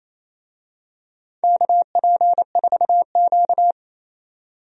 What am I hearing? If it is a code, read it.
Morse code, heard as a keyed tone, about 28 wpm, 700 Hz. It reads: KP4Q
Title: DXCC-100 Call Signs - 4 Characters - 28wpm